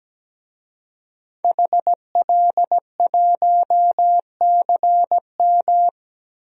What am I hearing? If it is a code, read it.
Morse code, heard as a keyed tone, about 17 wpm, 700 Hz. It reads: HL1CM
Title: DXCC-100 Call Signs - 5 Characters - 17wpm